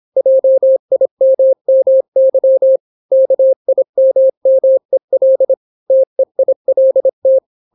Morse code, heard as a keyed tone, about 26 wpm, 530 Hz. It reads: JIMMY KIMMEL TEILT